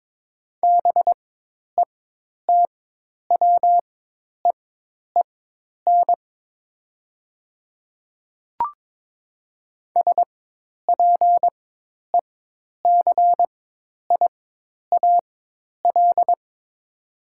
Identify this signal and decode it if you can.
Morse code, heard as a keyed tone, about 22 wpm, spaced as for 10 wpm, 700 Hz. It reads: BETWEEN SPECIAL
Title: 4x Character Spacing - Top 500 Words - 22wpm